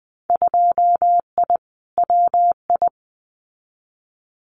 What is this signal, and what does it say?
Morse code, heard as a keyed tone, about 20 wpm, 700 Hz. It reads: 2I WI
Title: Winter Field Day Exchanges 20wpm